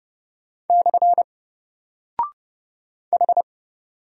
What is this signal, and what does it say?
Morse code, heard as a keyed tone, about 30 wpm, 700 Hz. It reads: / H
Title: Single Character (Letter:Number:Punctuation:Prosign) Alt 30wpm